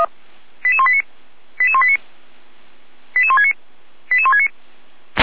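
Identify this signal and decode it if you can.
Five-tone selective call (ZVEI), heard as quick runs of five tones, tones of about 70 ms. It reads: 88189 88189 88169 88169
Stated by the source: ZVEI five-tone selective call received by SDR.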